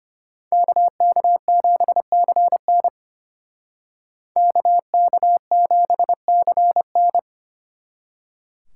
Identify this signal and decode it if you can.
Morse code, heard as a keyed tone, about 33 wpm, 700 Hz. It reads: KK7CN KK7CN